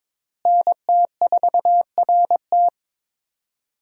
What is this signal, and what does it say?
Morse code, heard as a keyed tone, about 22 wpm, 700 Hz. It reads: NT4RT
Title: USA Call Signs - 5 Characters - Repeated 3x - 22wpm